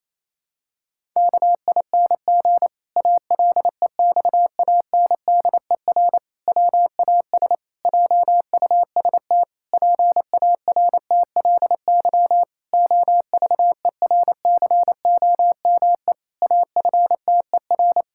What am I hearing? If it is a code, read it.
Morse code, heard as a keyed tone, about 28 wpm, 700 Hz. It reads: KING ALEXANDER WAS JUST PARTLY OVERCOME AFTER
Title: Pangram Challenge v2 - 28wpm